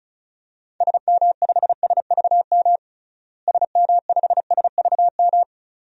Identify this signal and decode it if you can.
Morse code, heard as a keyed tone, about 35 wpm, 700 Hz. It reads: SM5SVM SM5SVM
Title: DXCC-100 Call Signs - 6 Characters - Repeated 2x - 35wpm